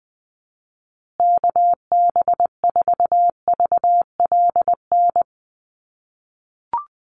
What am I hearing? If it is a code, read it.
Morse code, heard as a keyed tone, about 20 wpm, 700 Hz. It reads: KB4VLN
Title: USA Call Signs Repeated 2x 20wpm